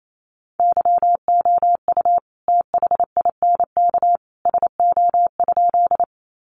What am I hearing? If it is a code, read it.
Morse code, heard as a keyed tone, about 28 wpm, 700 Hz. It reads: YOU THINK SO?
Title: Sentences from Top 200 Words 28wpm